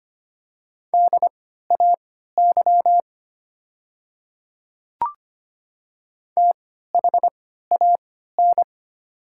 Morse code, heard as a keyed tone, about 25 wpm, spaced as for 14 wpm, 700 Hz. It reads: DAY THAN
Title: 3x Character Spacing - Top 100 Words - 25wpm